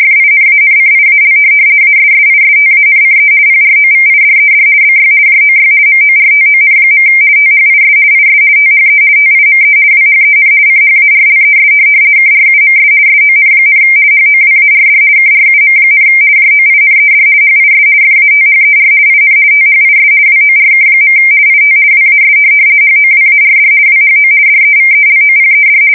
RTTY, about 50 baud. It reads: FINANCIAL ASSISTANCE PROVIDED BY JOY MORTON. THIS IS THE TIME WHEN CHARLES KRUMS SON, HOWARD KRUM, JOINED HIS FATHER IN THIS WORK. IT WAS HOWARD WHO DEVELOPED
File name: fsk170-50B/teletype-corp